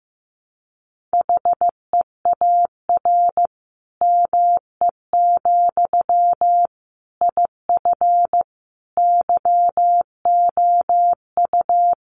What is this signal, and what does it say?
Morse code, heard as a keyed tone, about 15 wpm, 700 Hz. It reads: HEAR ME, IF YOU
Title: Sentences from Top 1500 Words 15wpm